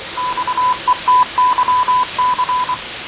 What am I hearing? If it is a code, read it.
Morse code, heard as a keyed tone, about 24 wpm, 1000 Hz, with noise. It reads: XETYC
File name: fav22-6824usb-20251029-1420z-voice-test-win10